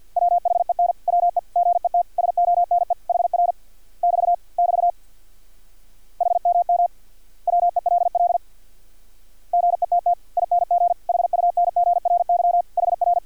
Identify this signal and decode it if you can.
Morse code, heard as a keyed tone, about 50 wpm, 700 Hz. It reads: WHEN WE GET SONE HR <BT> <BT> HMM WELL GETTING HUNGRY HR
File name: archived / CRSnet